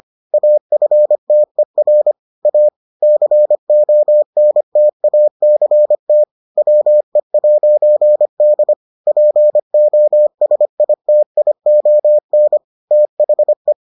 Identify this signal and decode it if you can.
Morse code, heard as a keyed tone, about 25 wpm, 600 Hz. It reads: AFTER A CONTACT WE'D POSITION THE